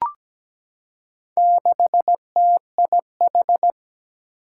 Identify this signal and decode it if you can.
Morse code, heard as a keyed tone, about 17 wpm, 700 Hz. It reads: 6TIH
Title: Four Challenging Letter-Number Combinations 17wpm